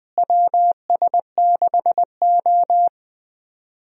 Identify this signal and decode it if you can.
Morse code, heard as a keyed tone, about 20 wpm, 700 Hz. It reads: WS6O
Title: USA Call Signs - 4 Characters - Repeated 3x - 20wpm